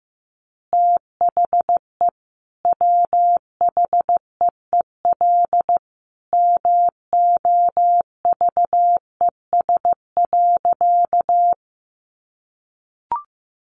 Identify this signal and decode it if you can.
Morse code, heard as a keyed tone, about 15 wpm, 700 Hz. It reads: THE WHEEL MOVES.
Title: Additional Easy Phrases 15wpm